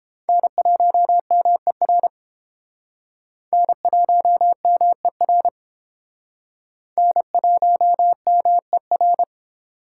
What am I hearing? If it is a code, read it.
Morse code, heard as a keyed tone, about 30 wpm, 700 Hz. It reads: N1MER N1MER N1MER